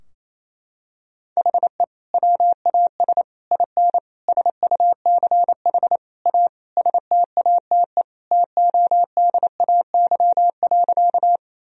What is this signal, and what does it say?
Morse code, heard as a keyed tone, about 28 wpm, 700 Hz. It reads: HE WAS IN SUCH A STATE TODAY.
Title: Sentences from Top 1000 Words - Review Part 2 - 28wpm